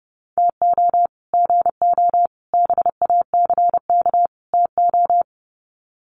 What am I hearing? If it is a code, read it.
Morse code, heard as a keyed tone, about 30 wpm, 700 Hz. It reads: TO GO BACK TO
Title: Sets of 4 Words 30wpm